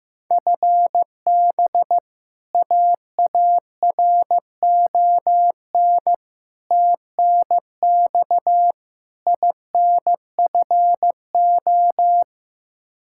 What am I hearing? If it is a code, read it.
Morse code, heard as a keyed tone, about 15 wpm, 700 Hz. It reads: FB AARON TNX INFO
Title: K1USN SST QSO Practice 15wpm